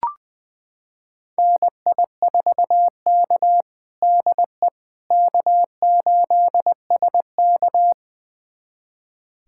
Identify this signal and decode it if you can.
Morse code, heard as a keyed tone, about 20 wpm, 700 Hz. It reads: NI4K DE K8SK
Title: USA Callsign DE Callsign - 4 Characters - 20wpm